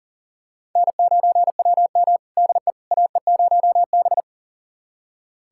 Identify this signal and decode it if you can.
Morse code, heard as a keyed tone, about 38 wpm, 700 Hz. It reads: N9WM DE AE0B
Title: USA Callsign DE Callsign - 4 Characters - 40wpm